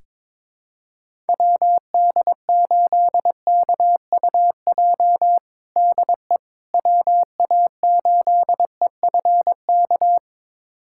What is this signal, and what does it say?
Morse code, heard as a keyed tone, about 21 wpm, 700 Hz. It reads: WD8KUJ DE WA8EFK
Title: USA Callsign DE Callsign 22wpm